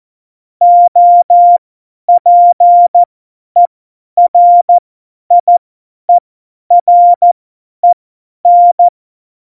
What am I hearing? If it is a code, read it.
Morse code, heard as a keyed tone, about 14 wpm, spaced as for 10 wpm, 700 Hz. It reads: OPERIEREN
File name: Morsekurs / Witze / Witz 18. Dezember, 50 F70